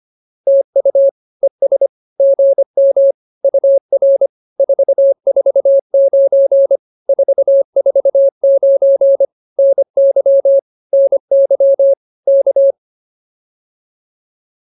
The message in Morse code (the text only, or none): TU ES GM UR 449 449 NY NY K